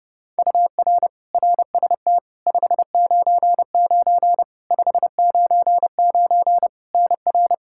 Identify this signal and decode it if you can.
Morse code, heard as a keyed tone, about 30 wpm, 700 Hz. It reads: UR RST 599 599 NR
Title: Practice QSO - Exchange 1 - Beginning - 30wpm